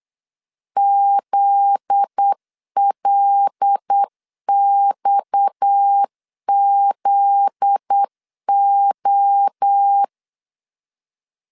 Morse code, heard as a keyed tone, about 8 wpm, 800 Hz. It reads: ZLXZO
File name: LEZIONI / 7 GF9Z / 7 GF9Z 8wpm 3213